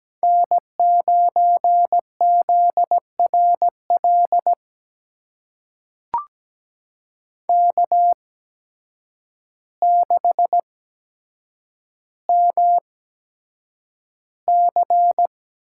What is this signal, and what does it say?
Morse code, heard as a keyed tone, about 17 wpm, spaced as for 8 wpm, 700 Hz. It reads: N9ZRL K6MC
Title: USA Call Signs - 5 Characters - 8x Character Spacing - 17wpm